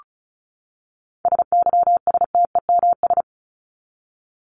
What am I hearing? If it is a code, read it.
Morse code, heard as a keyed tone, about 34 wpm, 700 Hz. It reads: SYSTEMS
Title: ES:S Suffix - Top 100 Words with Base Words 35wpm